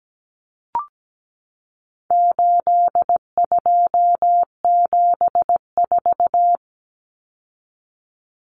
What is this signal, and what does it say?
Morse code, heard as a keyed tone, about 17 wpm, 700 Hz. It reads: 8274
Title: Numbers - 4-Digit 17wpm